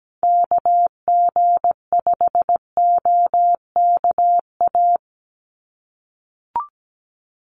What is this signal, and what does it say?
Morse code, heard as a keyed tone, about 17 wpm, 700 Hz. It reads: KG5OKA
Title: USA Call Signs 17wpm